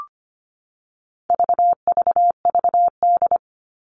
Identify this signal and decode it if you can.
Morse code, heard as a keyed tone, about 25 wpm, 700 Hz. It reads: VVVD
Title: All Cut Numbers - 4-Digit 25wpm